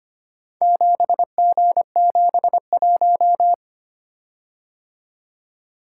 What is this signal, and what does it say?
Morse code, heard as a keyed tone, about 25 wpm, 700 Hz. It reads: 7G71
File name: practice / 25wpm / Four Letter-Number Combinations - 25wpm